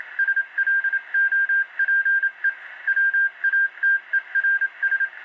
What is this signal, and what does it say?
Morse code, heard as a keyed tone, about 27 wpm, 1600 Hz. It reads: SLOPE WATERS